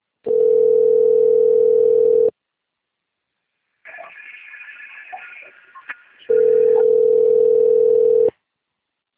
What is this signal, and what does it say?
Ringback tone (North America), 440 + 480 Hz.